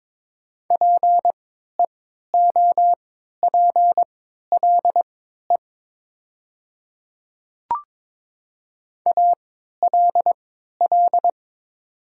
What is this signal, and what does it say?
Morse code, heard as a keyed tone, about 22 wpm, spaced as for 13 wpm, 700 Hz. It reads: PEOPLE ALL
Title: 3x Character Spacing - Top 100 Words - 22wpm